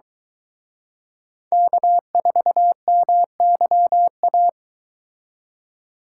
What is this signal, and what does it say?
Morse code, heard as a keyed tone, about 23 wpm, 700 Hz. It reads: K4MYA